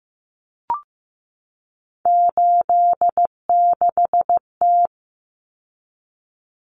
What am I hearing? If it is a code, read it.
Morse code, heard as a keyed tone, about 15 wpm, 700 Hz. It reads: 86T
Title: Common Cut Numbers - 3-Digit 15wpm